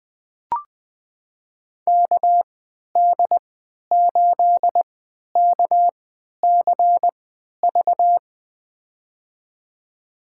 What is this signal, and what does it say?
Morse code, heard as a keyed tone, about 20 wpm, spaced as for 11 wpm, 700 Hz. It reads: KD8KCV
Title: USA Call Signs - 6 Characters - 3x Character Spacing - 20wpm